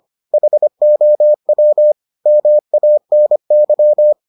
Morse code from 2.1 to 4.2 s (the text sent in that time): MANY